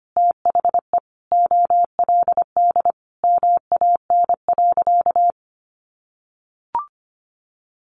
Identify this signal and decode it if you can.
Morse code, heard as a keyed tone, about 25 wpm, 700 Hz. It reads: THE OLD MAN.
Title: Sentences from Top 200 Words 25wpm